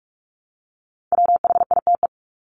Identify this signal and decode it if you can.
Morse code, heard as a keyed tone, about 44 wpm, 700 Hz. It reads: WHITE
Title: Top 500 Words 45wpm